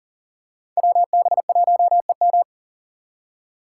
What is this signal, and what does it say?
Morse code, heard as a keyed tone, about 41 wpm, 700 Hz. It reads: WB1EM
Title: Code Groups - 5 Letter:Numbers - 40wpm